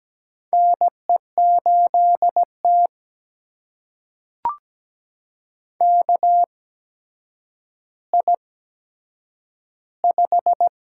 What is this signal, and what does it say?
Morse code, heard as a keyed tone, about 17 wpm, spaced as for 8 wpm, 700 Hz. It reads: NE8T KI5
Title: USA Call Signs - 4 Characters - 8x Character Spacing - 17wpm